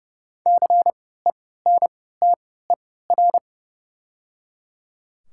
Morse code, heard as a keyed tone, about 30 wpm, spaced as for 17 wpm, 700 Hz. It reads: CENTER
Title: 3x Character Spacing - Top 500 Words - 30wpm